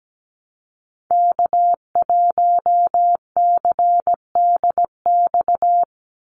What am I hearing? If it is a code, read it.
Morse code, heard as a keyed tone, about 17 wpm, 700 Hz. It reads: K1CDX
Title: Call Signs - 17wpm